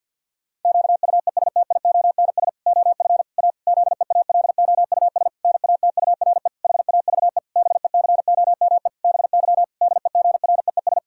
Morse code, heard as a keyed tone, about 50 wpm, 700 Hz. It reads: QUESTIONS OF A ZEALOUS NATURE HAVE BECOME BY DEGREES
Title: Pangram Challenge v2 - 50wpm